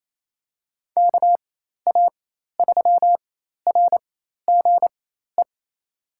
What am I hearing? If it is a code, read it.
Morse code, heard as a keyed tone, about 28 wpm, spaced as for 13 wpm, 700 Hz. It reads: KA3RGE
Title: USA Call Signs - 6 Characters - 4x Character Spacing - 28wpm